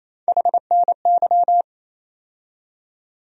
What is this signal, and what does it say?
Morse code, heard as a keyed tone, about 28 wpm, 700 Hz. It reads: HNY
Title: USA Call Sign Postfixes - 3 Letters 28wpm